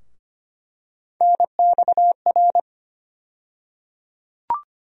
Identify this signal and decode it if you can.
Morse code, heard as a keyed tone, about 25 wpm, 700 Hz. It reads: NXR
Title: USA Call Sign Postfixes - 3 Letters 25wpm